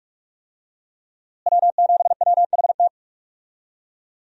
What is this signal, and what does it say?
Morse code, heard as a keyed tone, about 46 wpm, 700 Hz. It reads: W7WHT
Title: USA Call Signs - 5 Characters - Repeated 3x - 45wpm